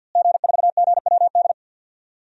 Morse code, heard as a keyed tone, about 48 wpm, 700 Hz. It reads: G4GWD
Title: DXCC-100 Call Signs - 5 Characters - Repeated 2x - 50wpm